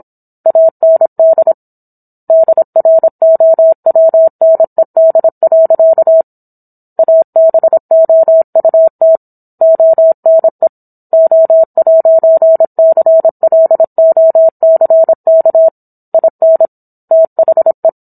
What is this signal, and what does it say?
Morse code, heard as a keyed tone, about 27 wpm, 650 Hz. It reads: AND DROWNED. ABOUT ONE O'CLOCK IN THE